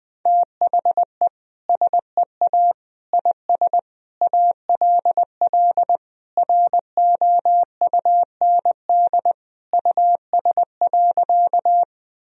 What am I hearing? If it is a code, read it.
Morse code, heard as a keyed tone, about 19 wpm, 700 Hz. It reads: THE SEA IS ALL ROUND US.
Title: Sentences from Top 1000 Words - Review Part 1 - 20wpm